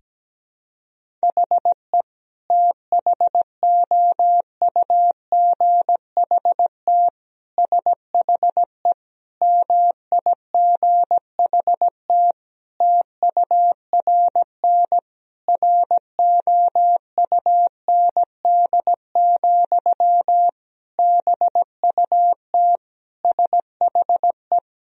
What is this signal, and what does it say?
Morse code, heard as a keyed tone, about 17 wpm, 700 Hz. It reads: HE THOUGHT SHE MIGHT TURN ROUND, BUT SHE
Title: Sentences from Top 1000 Words - Review Part 1 - 17wpm